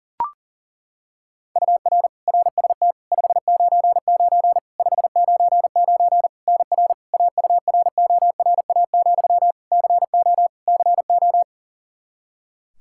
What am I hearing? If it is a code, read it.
Morse code, heard as a keyed tone, about 40 wpm, 700 Hz. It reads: UR RST 599 599 NR AURORA, CO CO
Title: K1USN SST QSO Practice 40wpm